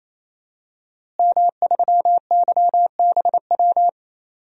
Morse code, heard as a keyed tone, about 28 wpm, 700 Hz. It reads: M3YBW
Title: DXCC-100 Call Signs - 5 Characters - Repeated 3x - 28wpm